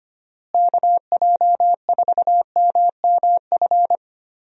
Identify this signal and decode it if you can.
Morse code, heard as a keyed tone, about 25 wpm, 700 Hz. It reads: KJ4MMF
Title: USA Call Signs - 6 Characters - Repeated 3x - 25wpm